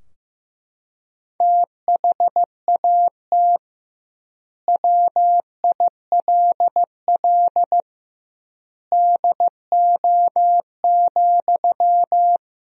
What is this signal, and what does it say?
Morse code, heard as a keyed tone, about 15 wpm, 700 Hz. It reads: THAT WILL DO,
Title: 2x Word-Spacing - Sentences from Top 100 Words - 15wpm